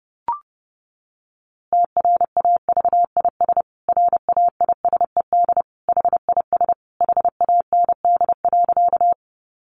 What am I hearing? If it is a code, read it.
Morse code, heard as a keyed tone, about 30 wpm, 700 Hz. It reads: TRAVIS RAISED HIS HAND.
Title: A Sound Of Thunder 30wpm